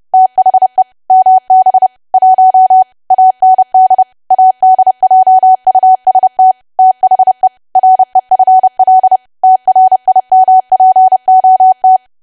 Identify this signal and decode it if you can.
Morse code, heard as a keyed tone, about 30 wpm, 750 Hz. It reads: THE MB 1 AND ADJUST THE REFL TRIMPOT
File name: cwa / W1AW-Level3 / 30WPM